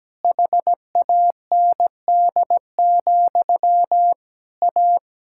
Morse code, heard as a keyed tone, about 17 wpm, 700 Hz. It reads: HAND, A
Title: Sentences from Top 1600 Words 17wpm